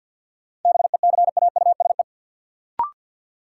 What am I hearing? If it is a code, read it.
Morse code, heard as a keyed tone, about 53 wpm, 700 Hz. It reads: BECAUSE
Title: Sets of 2 Words - New Word Drill - 50wpm